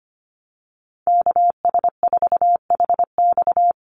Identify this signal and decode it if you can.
Morse code, heard as a keyed tone, about 25 wpm, 700 Hz. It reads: KS4HX
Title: USA Call Signs 25wpm